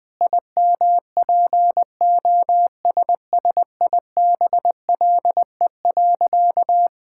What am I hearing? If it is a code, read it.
Morse code, heard as a keyed tone, about 20 wpm, 700 Hz. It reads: IMPOSSIBLE.
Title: Sentences with 6-Word Phrases - 20wpm